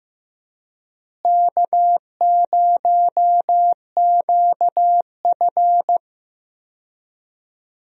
Morse code, heard as a keyed tone, about 15 wpm, 700 Hz. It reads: K0QF
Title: USA Call Signs - 4 Characters - Repeated 2x - 15wpm